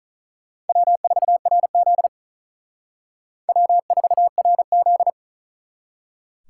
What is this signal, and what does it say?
Morse code, heard as a keyed tone, about 46 wpm, 700 Hz. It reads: W4RZ W4RZ